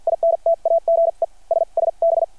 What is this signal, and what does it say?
Morse code, heard as a keyed tone, about 53 wpm, spaced as for 42 wpm, 660 Hz. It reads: INTAGE SSB
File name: archived / CRSnet